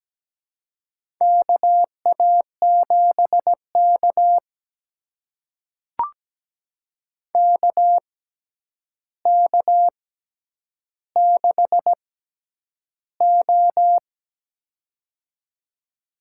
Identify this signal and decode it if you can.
Morse code, heard as a keyed tone, about 17 wpm, spaced as for 9 wpm, 700 Hz. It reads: KA7K KK6O
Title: USA Call Signs - 4 Characters - 6x Character Spacing - 17wpm